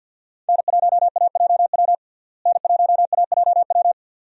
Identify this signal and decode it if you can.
Morse code, heard as a keyed tone, about 53 wpm, 700 Hz. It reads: N1AJW N1AJW
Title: USA Call Signs - 5 Characters - Repeated 2x - 50wpm